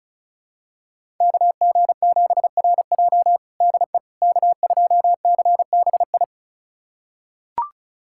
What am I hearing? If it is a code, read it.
Morse code, heard as a keyed tone, about 36 wpm, 700 Hz. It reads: KG7RJ DE K2CBI
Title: USA Callsign DE Callsign - 5 Characters - 35wpm